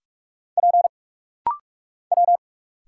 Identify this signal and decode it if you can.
Morse code, heard as a keyed tone, about 44 wpm, 700 Hz. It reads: P W
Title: Single Letter-Number Alt - Rapid-Fire 45wpm